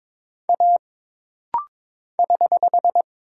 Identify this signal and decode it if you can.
Morse code, heard as a keyed tone, about 22 wpm, 700 Hz. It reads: A <HH>
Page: This rapid-fire timing is geared to push your ICR (Instant Character Recognition) skills.